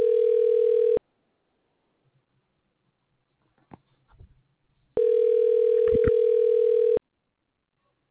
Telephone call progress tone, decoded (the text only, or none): ringback tone